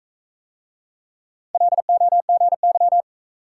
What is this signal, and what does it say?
Morse code, heard as a keyed tone, about 40 wpm, 700 Hz. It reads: LOGY